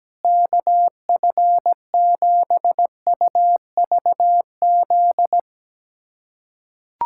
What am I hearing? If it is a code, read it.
Morse code, heard as a keyed tone, about 17 wpm, 700 Hz. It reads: KF7UVZ
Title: USA Callsign DE Callsign - 6 Characters - 17wpm